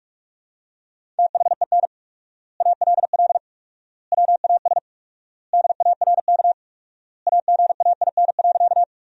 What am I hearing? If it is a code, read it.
Morse code, heard as a keyed tone, about 41 wpm, 690 Hz. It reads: THEN ALL WAS DARK AGAIN.
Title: 4x Word-Spacing - Sentences from Top 500 Words - 45wpm